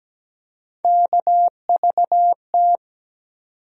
Morse code, heard as a keyed tone, about 17 wpm, 700 Hz. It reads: KVT
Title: USA Call Sign Postfixes - 3 Letters 17wpm